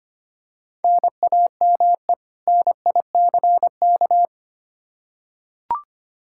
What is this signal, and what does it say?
Morse code, heard as a keyed tone, about 25 wpm, 700 Hz. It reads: NAME NICK
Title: Practice QSO - Exchange 1 - Beginning - 25wpm